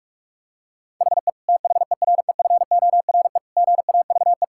Morse code, heard as a keyed tone, about 41 wpm, 700 Hz. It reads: HE THEREFORE GAVE
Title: Sentences from Top 1000 Words - Review Part 5 - 45wpm